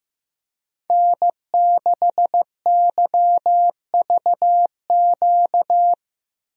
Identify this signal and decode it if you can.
Morse code, heard as a keyed tone, about 15 wpm, 700 Hz. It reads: N6YVQ
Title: USA Call Signs - 5 Characters - 15wpm